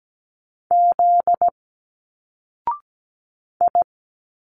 Morse code, heard as a keyed tone, about 17 wpm, 700 Hz. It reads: Z I
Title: Single Letters 17wpm